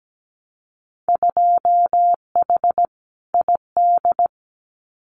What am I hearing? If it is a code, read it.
Morse code, heard as a keyed tone, about 17 wpm, 700 Hz. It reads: 2H ID
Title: Winter Field Day Exchanges 17wpm